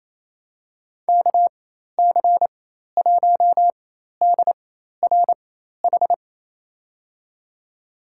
Morse code, heard as a keyed tone, about 28 wpm, spaced as for 13 wpm, 700 Hz. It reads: KC1DRH
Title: USA Call Signs - 6 Characters - 4x Character Spacing - 28wpm